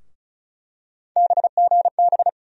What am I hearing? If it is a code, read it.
Morse code, heard as a keyed tone, about 34 wpm, 700 Hz. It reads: BGB
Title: USA Call Sign Postfixes - 3 Letters 35wpm